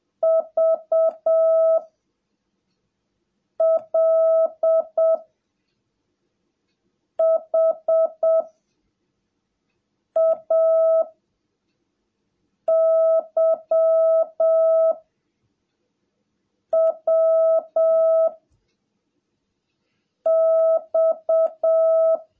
Morse code, heard as a keyed tone, about 7 wpm, 660 Hz. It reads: V L H A Y W X
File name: jh8cbh / 2021cw / cw17a